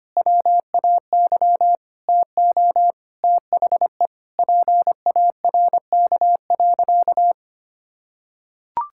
Morse code, heard as a keyed tone, about 25 wpm, 700 Hz. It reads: WAY TO THE PARK.
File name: practice / 25wpm / Sentences with 4-Word Phrases - 25wpm